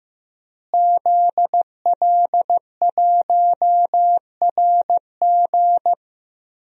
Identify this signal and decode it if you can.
Morse code, heard as a keyed tone, about 15 wpm, 700 Hz. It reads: ZL1RG
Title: DXCC-100 Call Signs - 5 Characters - Repeated 2x - 15wpm